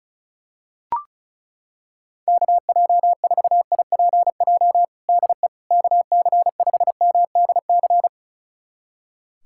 Morse code, heard as a keyed tone, about 34 wpm, 700 Hz. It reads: KJ4IPJ DE KC5MDC
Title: USA Callsign DE Callsign - 6 Characters - 35wpm